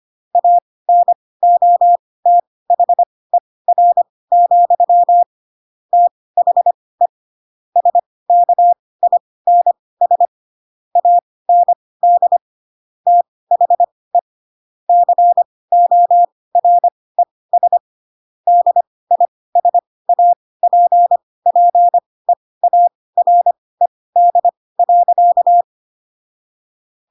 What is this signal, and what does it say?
Morse code, heard as a keyed tone, about 25 wpm, spaced as for 18 wpm, 700 Hz. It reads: ANOTHER, THE SKINS AND THE CORES DISAPPEARED.
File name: Pinocchio / Pinocchio ch7 12 25 ik1hge